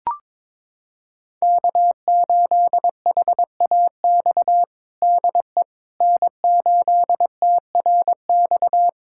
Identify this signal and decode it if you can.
Morse code, heard as a keyed tone, about 22 wpm, 700 Hz. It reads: K8HAX DE N8TRX
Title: USA Callsign DE Callsign - 5 Characters - 22wpm